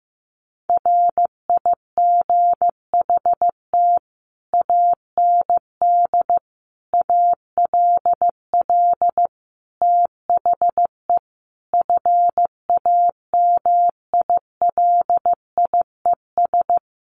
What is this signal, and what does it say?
Morse code, heard as a keyed tone, about 15 wpm, 700 Hz. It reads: RIGHT AND ALL THE FAMILIES
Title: A Sound Of Thunder 15wpm